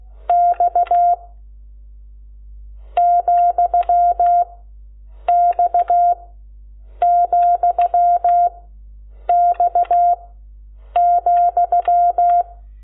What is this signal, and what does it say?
Morse code, heard as a keyed tone, about 16 wpm, 700 Hz. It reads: X , X , X ,
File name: Lecciones / CLASE Nº12